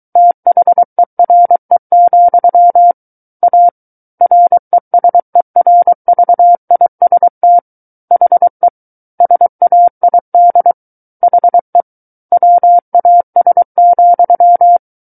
Morse code, heard as a keyed tone, about 23 wpm, 700 Hz. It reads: THERE, A RESERVIST HE SAID HE WAS,
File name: Book One / 14-Chapter 14 - 23 WPM